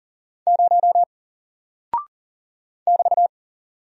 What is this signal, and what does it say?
Morse code, heard as a keyed tone, about 42 wpm, 700 Hz. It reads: 0 <BT>
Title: Single Character (Letter:Number:Punctuation:Prosign) 40wpm